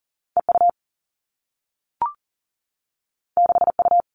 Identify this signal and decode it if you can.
Morse code, heard as a keyed tone, about 39 wpm, 700 Hz. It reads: EU 6U
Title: Two Challenging Letter-Number Combinations 40wpm